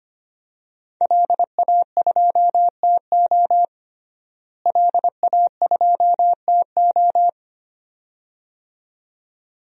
Morse code, heard as a keyed tone, about 25 wpm, 700 Hz. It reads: LA2TO LA2TO
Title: DXCC-100 Call Signs - 5 Characters - Repeated 2x - 25wpm